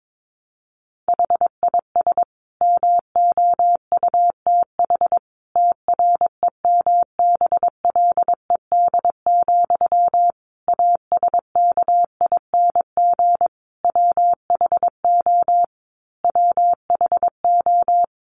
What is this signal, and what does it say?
Morse code, heard as a keyed tone, about 22 wpm, 700 Hz. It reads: HIS MOUTH TREMBLED, ASKING WHO WHO
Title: A Sound Of Thunder 22wpm